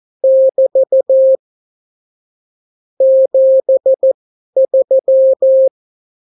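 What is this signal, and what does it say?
Morse code, heard as a keyed tone, about 14 wpm, spaced as for 11 wpm, 530 Hz. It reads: <BT> 73
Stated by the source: Example Spanish CW QSO from a Morse learning book, 14 wpm characters with 8 wpm spacing.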